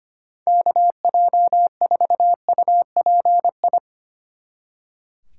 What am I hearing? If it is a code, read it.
Morse code, heard as a keyed tone, about 30 wpm, 700 Hz. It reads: KJ4UPI
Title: Speed Racing - USA 2x3 Call Signs 25wpm